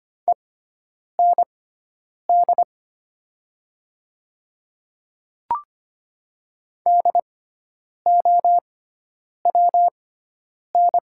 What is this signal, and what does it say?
Morse code, heard as a keyed tone, about 25 wpm, spaced as for 9 wpm, 700 Hz. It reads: END DOWN
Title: 6x Character Spacing - Top 300 Words - 25wpm